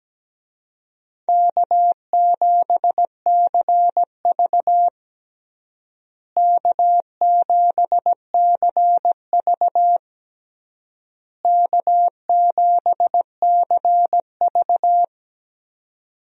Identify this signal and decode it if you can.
Morse code, heard as a keyed tone, about 17 wpm, 700 Hz. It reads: K7CV K7CV K7CV
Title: USA Call Signs - 4 Characters - Repeated 3x - 17wpm